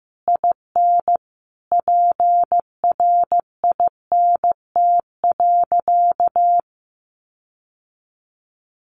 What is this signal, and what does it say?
Morse code, heard as a keyed tone, about 15 wpm, 700 Hz. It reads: IN PRINT.
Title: Sentences from Top 1400 Words 15wpm